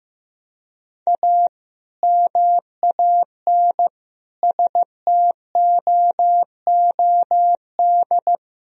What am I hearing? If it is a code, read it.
Morse code, heard as a keyed tone, about 15 wpm, 700 Hz. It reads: A MAN STOOD
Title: Sentences from Top 1000 Words - Review Part 2 - 15wpm